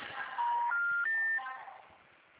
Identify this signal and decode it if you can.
Special information tone, with noise.